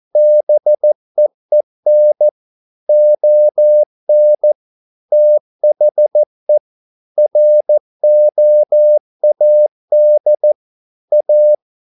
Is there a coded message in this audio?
Yes: Morse code from 0.2 to 11.5 s, heard as a keyed tone, about 14 wpm, 600 Hz. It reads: BEEN ON THE ROAD A